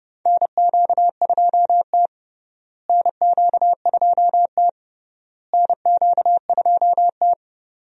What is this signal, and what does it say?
Morse code, heard as a keyed tone, about 30 wpm, 700 Hz. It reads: NQ2T NQ2T NQ2T
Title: USA Call Signs - 4 Characters - Repeated 3x - 30wpm